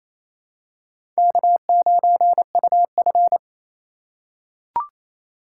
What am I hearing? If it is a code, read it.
Morse code, heard as a keyed tone, about 28 wpm, 700 Hz. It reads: K9UF
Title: USA Call Signs - 2x Character Spacing - 28wpm